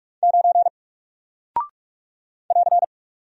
Morse code, heard as a keyed tone, about 44 wpm, 700 Hz. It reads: 9 <AR>
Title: Single Character (Letter:Number:Punctuation:Prosign) Alt 45wpm